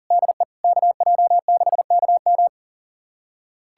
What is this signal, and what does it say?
Morse code, heard as a keyed tone, about 37 wpm, 700 Hz. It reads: DE KJ6KM
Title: USA Callsign DE Callsign - 5 Characters - 40wpm